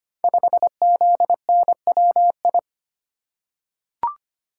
Morse code, heard as a keyed tone, about 25 wpm, 700 Hz. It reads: 5ZNWI
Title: Code Groups - 5 Letter:Numbers - 25wpm